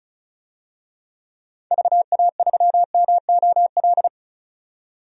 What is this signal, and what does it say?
Morse code, heard as a keyed tone, about 36 wpm, 700 Hz. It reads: VA3MOL
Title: DXCC-100 Call Signs - 6 Characters - 35wpm